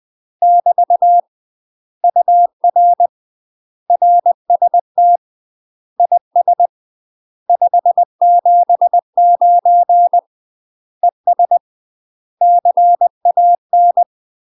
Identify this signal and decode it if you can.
Morse code, heard as a keyed tone, about 20 wpm, 700 Hz. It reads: <BT> UR RST IS 579 ES CAN
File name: UK MASTER QSOs TRAINING / 20WPM / UK MASTER QSOs 0034